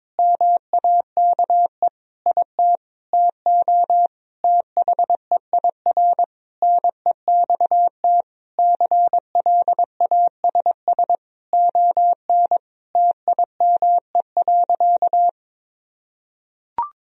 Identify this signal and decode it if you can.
Morse code, heard as a keyed tone, about 22 wpm, 700 Hz. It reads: MAKE IT TO THEIR NEXT CLASS ON TIME.
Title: Sentences from Top 1000 Words - Review Part 2 - 22wpm